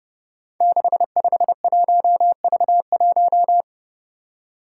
Morse code, heard as a keyed tone, about 30 wpm, 700 Hz. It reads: 651V1
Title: Five Challenging Letter-Number Combinations 30wpm